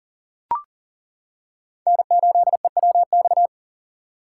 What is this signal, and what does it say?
Morse code, heard as a keyed tone, about 37 wpm, 700 Hz. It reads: N8EWX
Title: K1USN SST QSO Practice - 40wpm